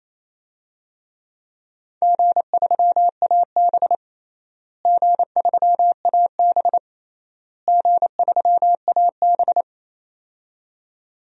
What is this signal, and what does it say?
Morse code, heard as a keyed tone, about 28 wpm, 700 Hz. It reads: G3AB G3AB G3AB
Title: DXCC-100 Call Signs - 4 Characters - Repeated 3x - 28wpm